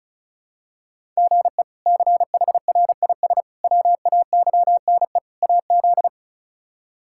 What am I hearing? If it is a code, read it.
Morse code, heard as a keyed tone, about 34 wpm, 700 Hz. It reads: GE CHRIS WAYNE AZ
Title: K1USN SST QSO Practice - 35wpm